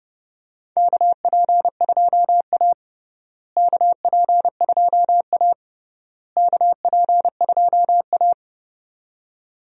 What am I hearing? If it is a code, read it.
Morse code, heard as a keyed tone, about 30 wpm, 700 Hz. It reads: KP2A KP2A KP2A
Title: DXCC-100 Call Signs - 4 Characters - Repeated 3x - 30wpm